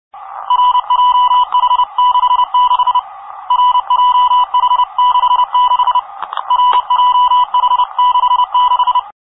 Morse code, heard as a keyed tone, about 30 wpm, 990 Hz. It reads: RJH66 RJH66 RJH66